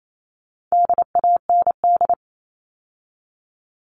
Morse code, heard as a keyed tone, about 28 wpm, 700 Hz. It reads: DAND
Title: All Cut Numbers - 4-Digit 28wpm